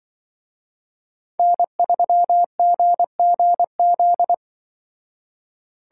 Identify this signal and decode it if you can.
Morse code, heard as a keyed tone, about 24 wpm, 700 Hz. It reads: N3GGZ
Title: Speed Racing - USA 1x3 Call Signs 20wpm